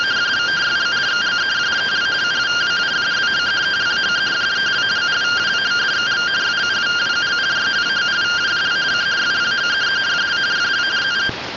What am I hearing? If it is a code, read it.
Two-tone teleprinter signal, about 170 Hz shift.
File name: findings / CT 205 ax mips, iDose 3 0004.gen-10 rtty rx